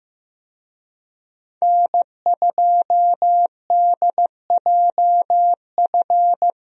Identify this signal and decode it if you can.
Morse code, heard as a keyed tone, about 15 wpm, 700 Hz. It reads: N2DJF
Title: USA Call Signs - 5 Characters - Repeated 3x - 15wpm